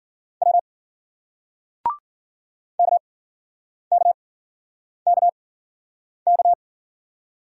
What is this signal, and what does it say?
Morse code, heard as a keyed tone, about 53 wpm, spaced as for 9 wpm, 700 Hz. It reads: W KKKK